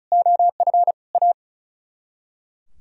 Morse code, heard as a keyed tone, about 45 wpm, 700 Hz. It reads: OF A